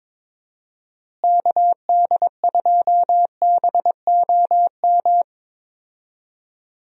Morse code, heard as a keyed tone, about 22 wpm, 700 Hz. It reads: KD2BOM
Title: USA Call Signs - 6 Characters - Repeated 3x - 22wpm